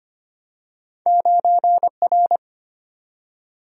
Morse code, heard as a keyed tone, about 25 wpm, 700 Hz. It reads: 9R
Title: Single-Letter Single-Number 25wpm